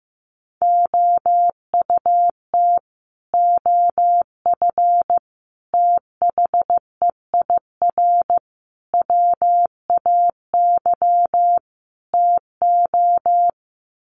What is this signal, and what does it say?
Morse code, heard as a keyed tone, about 15 wpm, 700 Hz. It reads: OUT OF THEIR WAY TO
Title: Sets of 5 Words Encore2 15wpm